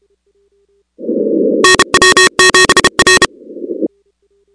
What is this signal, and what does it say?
Morse code, heard as a keyed tone, about 32 wpm, 1850 Hz, with noise. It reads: NW7R
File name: contests / 2019-cw-wpx-kt5j-k5pi- -2330 0-6